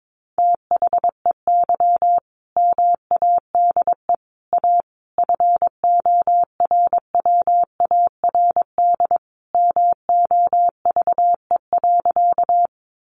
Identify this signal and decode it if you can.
Morse code, heard as a keyed tone, about 22 wpm, 700 Hz. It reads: THEY MADE A FORWARD MOVE.
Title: Sentences from Top 1300 Words 22wpm